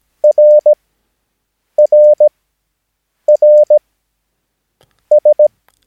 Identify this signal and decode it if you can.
Morse code, heard as a keyed tone, about 17 wpm, spaced as for 6 wpm, 600 Hz. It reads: RRRS
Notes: Övningsmaterial med alfabetet A till Ö samt siffror 0 till 9 enligt International/Continental Morse Code.